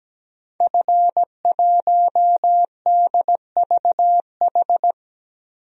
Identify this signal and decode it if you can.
Morse code, heard as a keyed tone, about 17 wpm, 700 Hz. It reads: F1DVH
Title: DXCC-100 Call Signs - 5 Characters - Repeated 2x - 17wpm